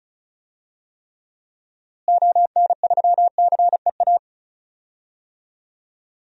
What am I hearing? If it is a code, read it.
Morse code, heard as a keyed tone, about 36 wpm, 700 Hz. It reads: ON3CEA
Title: DXCC-100 Call Signs - 6 Characters - Repeated 3x - 35wpm